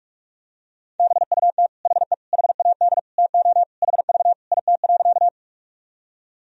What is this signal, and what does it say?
Morse code, heard as a keyed tone, about 44 wpm, 700 Hz. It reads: BUT HE HAD TO HV IT.
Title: Sentences from Top 100 Words and Abbreviations 45wpm